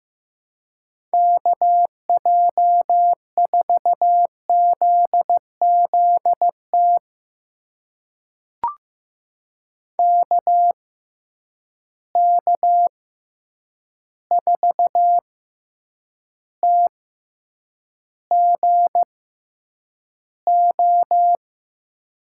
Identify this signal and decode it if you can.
Morse code, heard as a keyed tone, about 15 wpm, spaced as for 8 wpm, 700 Hz. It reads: KJ4ZZT KK4TGO
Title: USA Call Signs - 6 Characters - 6x Character Spacing - 15wpm